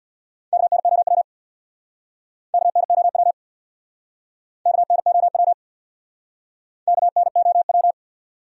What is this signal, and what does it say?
Morse code, heard as a keyed tone, about 68 wpm, 700 Hz. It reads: KNOW KNOW KNOW KNOW